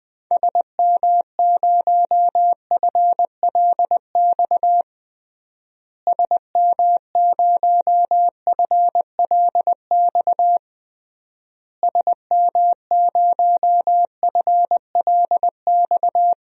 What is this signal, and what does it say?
Morse code, heard as a keyed tone, about 21 wpm, 700 Hz. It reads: SM0FLX SM0FLX SM0FLX
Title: DXCC-100 Call Signs - 6 Characters - Repeated 3x - 20wpm